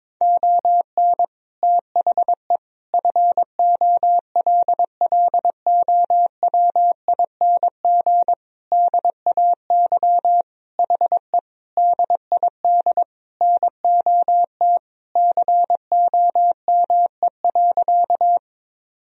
Morse code, heard as a keyed tone, about 22 wpm, 700 Hz. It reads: ON THE FOLLOWING DAY HE DID NOT COME.
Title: Sentences from Top 1000 Words - Review Part 4 - 22wpm